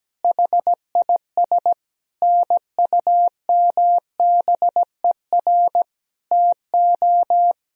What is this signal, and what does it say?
Morse code, heard as a keyed tone, about 17 wpm, 700 Hz. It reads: HIS NUMBER TO
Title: Sentences from Top 1000 Words - Review Part 1 - 17wpm